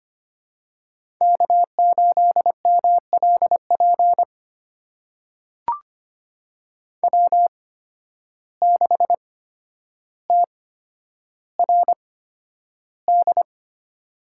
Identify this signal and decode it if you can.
Morse code, heard as a keyed tone, about 25 wpm, spaced as for 11 wpm, 700 Hz. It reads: K8MLP W6TRD
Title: USA Call Signs - 5 Characters - 8x Character Spacing - 25wpm